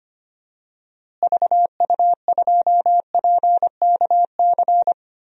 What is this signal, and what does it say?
Morse code, heard as a keyed tone, about 25 wpm, 700 Hz. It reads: VU2PKC
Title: DXCC-100 Call Signs - 6 Characters - Repeated 3x - 25wpm